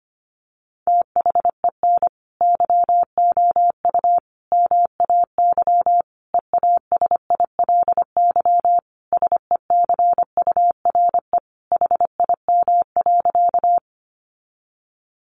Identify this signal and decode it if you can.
Morse code, heard as a keyed tone, about 25 wpm, 700 Hz. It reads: THEN YOU MAY EASILY SECURE HIM.
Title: Sentences from Top 1200 Words 25wpm